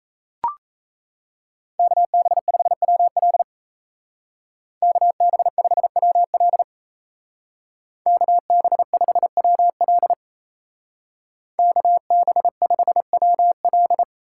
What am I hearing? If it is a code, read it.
Morse code, heard as a keyed tone, about 34 wpm, 700 Hz. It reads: KB5WL KB5WL KB5WL KB5WL
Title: Speed Racing - USA 2x2 Call Signs 28wpm